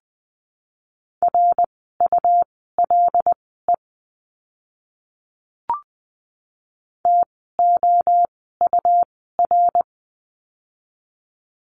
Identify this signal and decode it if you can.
Morse code, heard as a keyed tone, about 20 wpm, spaced as for 15 wpm, 700 Hz. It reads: RULE TOUR
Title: 101 Lesson 17f - Letter u - Words 20wpm